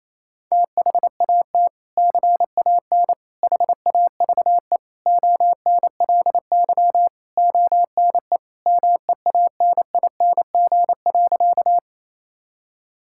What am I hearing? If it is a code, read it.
Morse code, heard as a keyed tone, about 28 wpm, 700 Hz. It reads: THAT CAN HAVE ONLY ONE MEANING.
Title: Sentences from Top 1000 Words - Review Part 6 - 28wpm